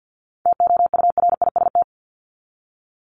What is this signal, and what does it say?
Morse code, heard as a keyed tone, about 50 wpm, 700 Hz. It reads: TOURIST
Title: Top 1501-1750 Words 50wpm